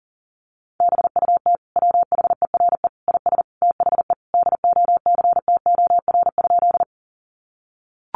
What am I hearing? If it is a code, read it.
Morse code, heard as a keyed tone, about 40 wpm, 700 Hz. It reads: BUT WHERE IS THE DOCTOR?
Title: Sentences from Top 1500 Words 40wpm